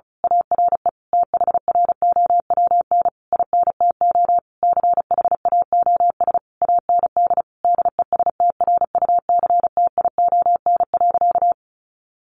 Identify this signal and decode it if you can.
Morse code, heard as a keyed tone, about 34 wpm, 700 Hz. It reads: ARE THROWN INTO CHAOS AND DESTRUCTION.
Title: A Sound Of Thunder 35wpm